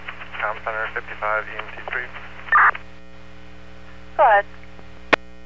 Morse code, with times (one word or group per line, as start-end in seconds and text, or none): none